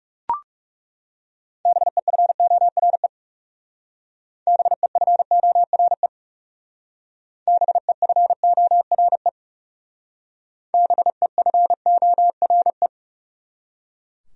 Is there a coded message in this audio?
Yes: Morse code from 1.7 to 12.9 s, heard as a keyed tone, about 38 wpm, 700 Hz. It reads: BEFORE BEFORE BEFORE BEFORE